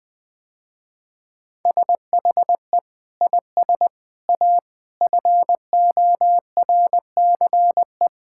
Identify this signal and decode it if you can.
Morse code, heard as a keyed tone, about 21 wpm, 700 Hz. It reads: SHE IS A FORCE
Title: Sentences with 8-Word Phrases - Spoken Beginning - 20wpm